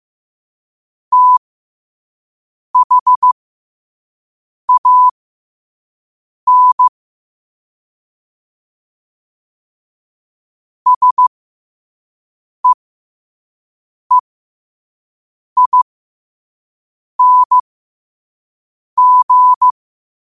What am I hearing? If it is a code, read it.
Morse code, heard as a keyed tone, about 15 wpm, spaced as for 5 wpm, 1000 Hz. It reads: THAN SEEING
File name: qotdfiles / 5wpm